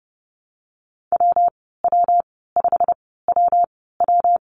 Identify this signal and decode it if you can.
Morse code, heard as a keyed tone, about 30 wpm, spaced as for 17 wpm, 700 Hz. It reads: WW5WW
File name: course / 30:10wpm / 116 Lesson 19g - Letter w - Call Signs 30wpm